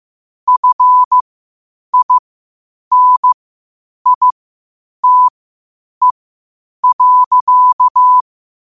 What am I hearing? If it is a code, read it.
Morse code, heard as a keyed tone, about 15 wpm, spaced as for 8 wpm, 1000 Hz. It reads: FINITE.
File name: qotdfiles / 8wpm